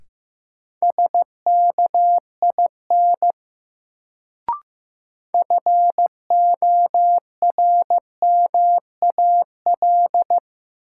Morse code, heard as a keyed tone, about 15 wpm, 700 Hz. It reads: SKIN FORMAL
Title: Sentences from Top 1300 Words - New Word Drill - 15wpm